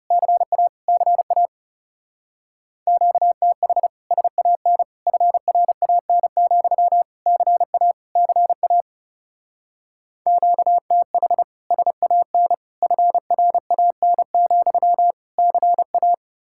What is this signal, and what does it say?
Morse code, heard as a keyed tone, about 39 wpm, 700 Hz. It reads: CA CA QTH SAN FRAN, CA CA QTH SAN FRAN, CA